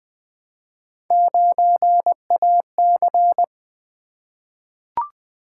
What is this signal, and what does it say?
Morse code, heard as a keyed tone, about 20 wpm, 700 Hz. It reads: 9AC
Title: Three Letter-Number Combinations - 20wpm